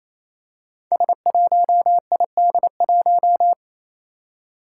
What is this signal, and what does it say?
Morse code, heard as a keyed tone, about 27 wpm, 700 Hz. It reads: S1ID1
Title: Five Challenging Letter-Number Combinations 28wpm